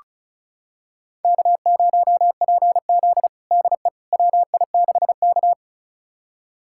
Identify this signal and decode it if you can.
Morse code, heard as a keyed tone, about 36 wpm, 700 Hz. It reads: K0PZ DE WI6K
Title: USA Callsign DE Callsign - 4 Characters - 35wpm